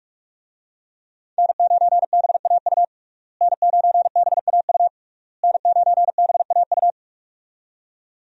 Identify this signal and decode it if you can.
Morse code, heard as a keyed tone, about 44 wpm, 700 Hz. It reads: N9BAU N9BAU N9BAU
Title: USA Call Signs - 5 Characters - Repeated 3x - 45wpm